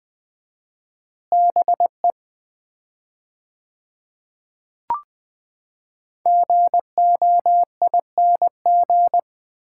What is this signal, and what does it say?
Morse code, heard as a keyed tone, about 20 wpm, 700 Hz. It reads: BE GOING
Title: 4x Word-Spacing - Sets of 3 Words - 20wpm